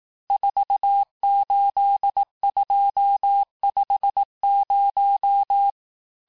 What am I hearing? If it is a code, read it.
Morse code, heard as a keyed tone, about 18 wpm, 800 Hz. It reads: 48250
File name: cours-du-lundi files / cours-1588-texte-chiffre-vitesse-17MPM